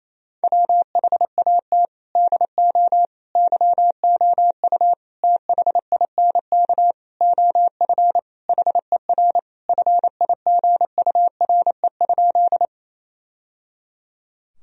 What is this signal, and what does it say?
Morse code, heard as a keyed tone, about 28 wpm, 700 Hz. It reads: WHAT DO YOU THINK OF HER FIGURE?
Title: Sentences from Top 1000 Words - Review Part 2 - 28wpm